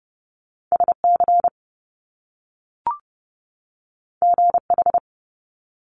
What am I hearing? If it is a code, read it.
Morse code, heard as a keyed tone, about 30 wpm, 700 Hz. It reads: S C G H
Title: Top Two Letter Combinations in Words 30wpm